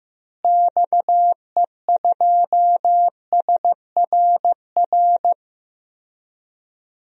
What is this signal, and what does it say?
Morse code, heard as a keyed tone, about 15 wpm, 700 Hz. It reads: XE2SRR
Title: DXCC-100 Call Signs - 6 Characters - Repeated 3x - 15wpm